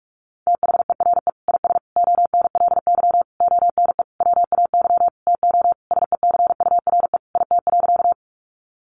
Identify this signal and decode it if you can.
Morse code, heard as a keyed tone, about 44 wpm, 700 Hz. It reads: THERE IS ONLY ONE WAY TO SECURE IT.
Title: Sentences from Top 1200 Words 45wpm